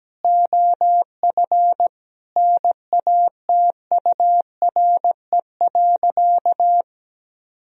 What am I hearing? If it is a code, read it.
Morse code, heard as a keyed tone, about 17 wpm, 700 Hz. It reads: OF NATURE.
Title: Sentences from Top 1000 Words - Review Part 4 - 17wpm